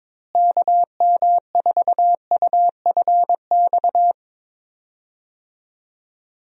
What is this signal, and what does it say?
Morse code, heard as a keyed tone, about 22 wpm, 700 Hz. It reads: KM4UFX
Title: USA Call Signs - 6 Characters - Repeated 2x - 22wpm